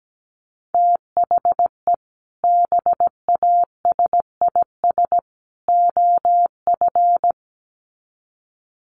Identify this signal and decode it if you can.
Morse code, heard as a keyed tone, about 17 wpm, 700 Hz. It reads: THE BASIS OF
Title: Sets of 3 Words 17wpm